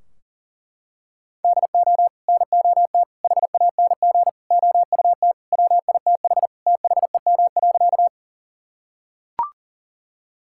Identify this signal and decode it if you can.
Morse code, heard as a keyed tone, about 40 wpm, 700 Hz. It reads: DO NOT HANG OUT WITH THEM.
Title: Sentences from Top 1600 Words 40wpm